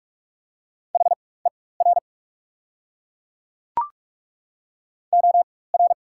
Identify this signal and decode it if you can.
Morse code, heard as a keyed tone, about 42 wpm, spaced as for 21 wpm, 690 Hz. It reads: HER OR
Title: 4x Character Spacing - Top 300 Words - 45wpm